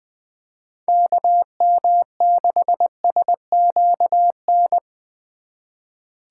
Morse code, heard as a keyed tone, about 21 wpm, 700 Hz. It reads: KM6SQN
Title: USA Call Signs - 6 Characters - 2x Character Spacing - 20wpm